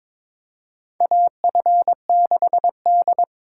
Morse code, heard as a keyed tone, about 21 wpm, 700 Hz. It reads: AF6D
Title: USA Callsign DE Callsign - 4 Characters - 22wpm